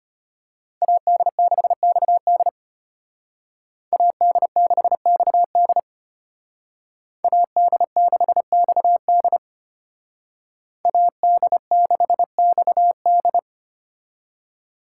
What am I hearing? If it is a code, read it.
Morse code, heard as a keyed tone, about 31 wpm, 700 Hz. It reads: AD6XD AD6XD AD6XD AD6XD